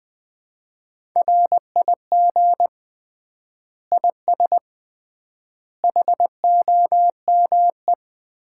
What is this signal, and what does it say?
Morse code, heard as a keyed tone, about 20 wpm, 700 Hz. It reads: RIG IS HOME
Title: Common QSO Phrases - 3x Word-Spacing - 20wpm